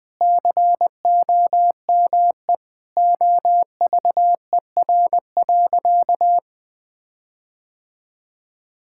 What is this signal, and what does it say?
Morse code, heard as a keyed tone, about 20 wpm, 700 Hz. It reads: COME OVER.
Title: Sentences with 4-Word Phrases - 20wpm